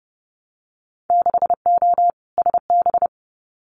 Morse code, heard as a keyed tone, about 32 wpm, 700 Hz. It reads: 6O SB
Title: Winter Field Day Exchanges Encore 30wpm